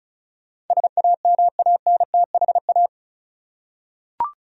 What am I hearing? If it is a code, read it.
Morse code, heard as a keyed tone, about 34 wpm, 700 Hz. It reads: SAMANTHA
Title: USA Common Names - 35wpm